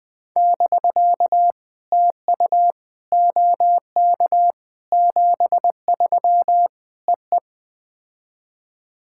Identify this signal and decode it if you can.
Morse code, heard as a keyed tone, about 21 wpm, 700 Hz. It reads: <BK> TU OK 73 EE
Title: POTA QSO Practice - 20wpm